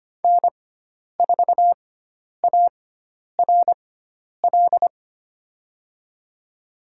Morse code, heard as a keyed tone, about 25 wpm, spaced as for 10 wpm, 700 Hz. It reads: N4ARL
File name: Lesson 16g - Letter c - Call Signs 25wpm